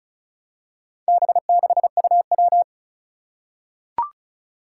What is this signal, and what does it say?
Morse code, heard as a keyed tone, about 36 wpm, 700 Hz. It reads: B6UW
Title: Four Challenging Letter-Number Combinations 35wpm